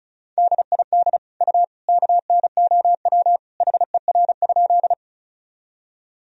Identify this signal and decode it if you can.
Morse code, heard as a keyed tone, about 36 wpm, 700 Hz. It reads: DID U KNOW HER?
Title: Sentences from Top 100 Words and Abbreviations 35wpm